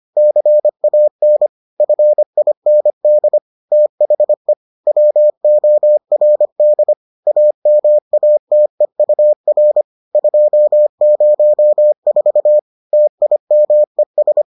Morse code, heard as a keyed tone, about 25 wpm, 600 Hz. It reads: CAN FIND THE WORD AMATEUR 204 TIMES